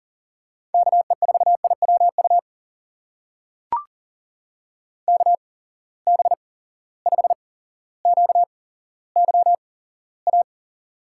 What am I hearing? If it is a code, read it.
Morse code, heard as a keyed tone, about 39 wpm, spaced as for 17 wpm, 700 Hz. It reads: KE4IWU KB5QYA
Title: USA Call Signs - 6 Characters - 8x Character Spacing - 40wpm